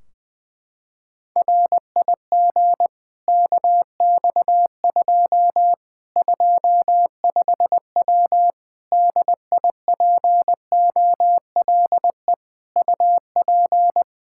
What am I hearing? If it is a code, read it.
Morse code, heard as a keyed tone, about 20 wpm, 700 Hz. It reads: RIG KX2 25W DIPOLE UP
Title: K1USN SST QSO Practice 20wpm